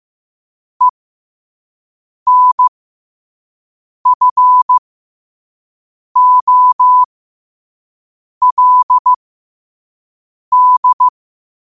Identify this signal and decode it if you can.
Morse code, heard as a keyed tone, about 15 wpm, spaced as for 5 wpm, 1000 Hz. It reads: ENFOLD
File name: qotdfiles / 5wpm